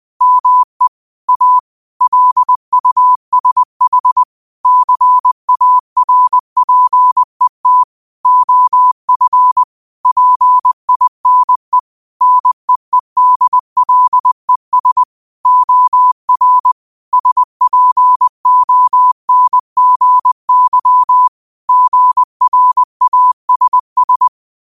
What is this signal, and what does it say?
Morse code, heard as a keyed tone, about 20 wpm, 1000 Hz. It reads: ME A LUSH CARPET OF PINE NEEDLES OR SPONGY GRASS
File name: qotdfiles / 20wpm